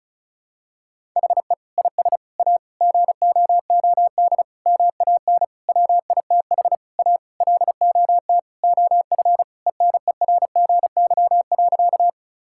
Morse code, heard as a keyed tone, about 36 wpm, 700 Hz. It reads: HE IS A GOOD MAN WITH A LOT OF ENERGY.
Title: Sentences from Top 1000 Words - Review Part 6 - 35wpm